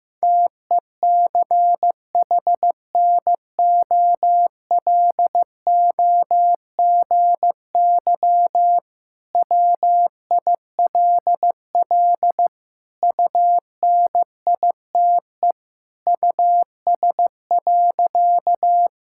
Morse code, heard as a keyed tone, about 15 wpm, 700 Hz. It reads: TECHNOLOGY WILL UNITE US.
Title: Sentences from Top 1000 Words - Review Part 5 - 15wpm